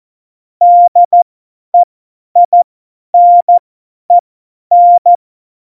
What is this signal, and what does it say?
Morse code, heard as a keyed tone, about 14 wpm, spaced as for 10 wpm, 700 Hz. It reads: DEINEN
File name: Morsekurs / Witze / Witz 10. Oktober 17, Tempo 50